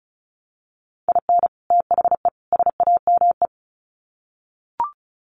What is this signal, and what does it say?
Morse code, heard as a keyed tone, about 34 wpm, 700 Hz. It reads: IN THE SAME
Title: Sets of 3 Words 35wpm